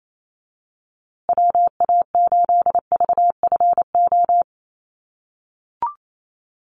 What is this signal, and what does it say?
Morse code, heard as a keyed tone, about 28 wpm, 700 Hz. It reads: WA8VFO
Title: USA Call Signs Repeated 3x 28wpm